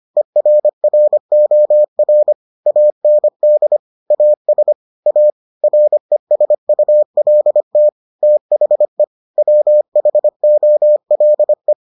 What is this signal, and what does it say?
Morse code, heard as a keyed tone, about 25 wpm, 600 Hz. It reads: ERROR AND AS A RESULT THE WHOLE